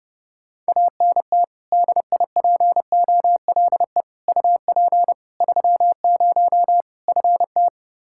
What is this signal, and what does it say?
Morse code, heard as a keyed tone, about 30 wpm, 700 Hz. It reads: ANT DIPOLE UP 30 FT
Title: K1USN SST QSO Practice 30wpm